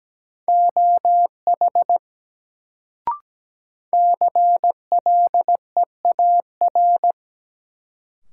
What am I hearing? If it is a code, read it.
Morse code, heard as a keyed tone, about 17 wpm, 700 Hz. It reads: OH CLEAR
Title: Sentences from Top 500 Words - New Word Drill - 17wpm